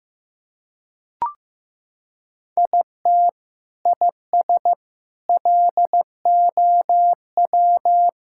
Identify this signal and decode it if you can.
Morse code, heard as a keyed tone, about 15 wpm, 700 Hz. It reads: IT IS LOW
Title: Sentences from Top 1000 Words - Review Part 5 - 15wpm